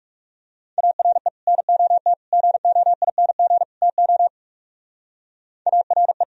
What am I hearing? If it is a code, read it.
Morse code, heard as a keyed tone, about 36 wpm, 700 Hz. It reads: ARE NOT GOING TO ARE
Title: Speed-Racing - Sets of 4 Words 30wpm